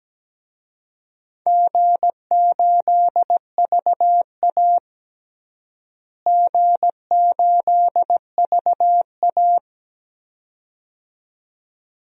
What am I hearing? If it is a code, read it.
Morse code, heard as a keyed tone, about 17 wpm, 700 Hz. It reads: G8VA G8VA
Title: DXCC-100 Call Signs - 4 Characters - Repeated 2x - 17wpm